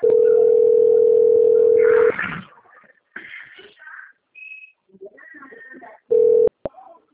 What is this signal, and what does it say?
Ringback tone (North America), 440 + 480 Hz.
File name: out- - - - - .382